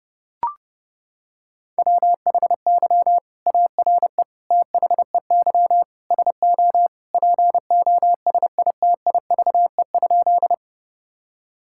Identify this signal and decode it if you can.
Morse code, heard as a keyed tone, about 30 wpm, 700 Hz. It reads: WHY ARE THEY SO POSITIVE?
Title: Sentences from Top 1000 Words - Review Part 5 - 30wpm